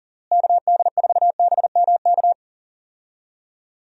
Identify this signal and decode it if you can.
Morse code, heard as a keyed tone, about 40 wpm, 700 Hz. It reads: KD4BMK
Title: USA Call Signs - 6 Characters - Repeated 3x - 40wpm